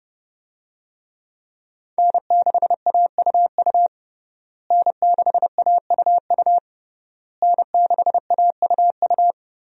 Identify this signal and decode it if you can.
Morse code, heard as a keyed tone, about 32 wpm, 700 Hz. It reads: N6AUU N6AUU N6AUU
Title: USA Call Signs - 5 Characters - Repeated 3x - 30wpm